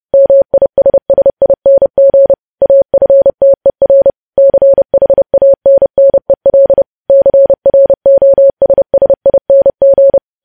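Morse code, heard as a keyed tone, about 30 wpm, 550 Hz. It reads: MISSING AFTER CHANNEL CROSSING
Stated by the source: News headlines sent in Morse at 30 wpm, preceded by VVV.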